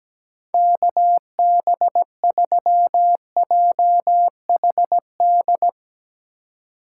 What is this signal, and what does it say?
Morse code, heard as a keyed tone, about 17 wpm, 700 Hz. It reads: KB3JHD
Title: USA Call Signs - 6 Characters - 4x Character Spacing - 17wpm